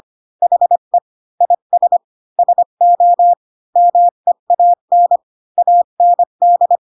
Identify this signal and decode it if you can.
Morse code, heard as a keyed tone, about 25 wpm, 700 Hz. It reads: HE IS SO MEAN AND
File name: Pinocchio / Pinocchio ch3 20 25 ik1hge